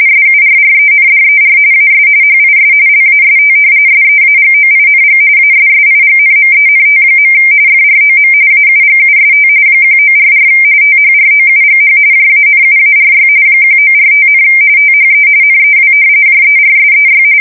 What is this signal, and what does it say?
RTTY, about 45.45 baud. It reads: SURRENDER AS HIS LAST SEAGOING ASSIGNMENT. FROM AUGUST 1944 THROUGH JANUARY 1945, THE NEW JERSEY
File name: fsk170 / battleshipnj